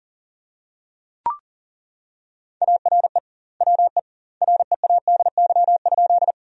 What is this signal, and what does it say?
Morse code, heard as a keyed tone, about 40 wpm, 700 Hz. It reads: ARE WE READY?
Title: 2x Word-Spacing - Sentences from Top 400 Words - 40wpm